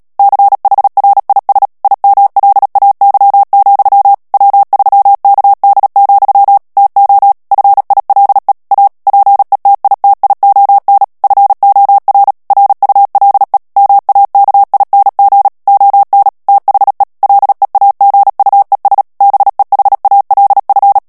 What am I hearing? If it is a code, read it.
Morse code, heard as a keyed tone, about 37 wpm, 800 Hz. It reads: CHRIS IMLAY, W3KD, TO FILE A PETITION FOR RULE MAKING ON THE LEAGUES BEHALF
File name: morse mp3 / ANGLAIS texte clair 3 vitesse 35 mots minute